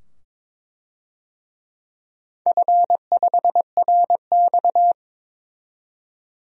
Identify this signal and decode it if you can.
Morse code, heard as a keyed tone, about 22 wpm, 700 Hz. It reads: F5RX
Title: DXCC-100 Call Signs - 4 Characters - 22wpm